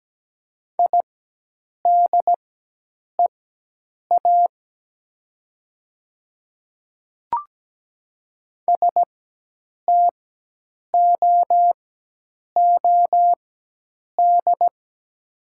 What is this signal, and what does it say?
Morse code, heard as a keyed tone, about 17 wpm, spaced as for 8 wpm, 690 Hz. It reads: IDEA STOOD
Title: 4x Character Spacing - Top 500 Words - 17wpm